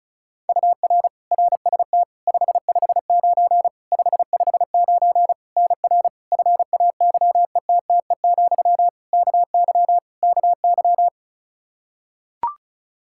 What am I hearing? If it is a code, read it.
Morse code, heard as a keyed tone, about 36 wpm, 700 Hz. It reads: UR RST 559 559 NR FAYETTE, KY KY
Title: Practice QSO - Exchange 1 - Beginning - 35wpm